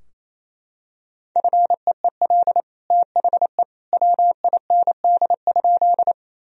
Morse code, heard as a keyed tone, about 28 wpm, 700 Hz. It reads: FEEL THE WIND?
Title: Sentences from Top 1000 Words - Review Part 2 - 28wpm